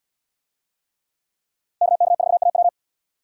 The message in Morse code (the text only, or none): KK4AW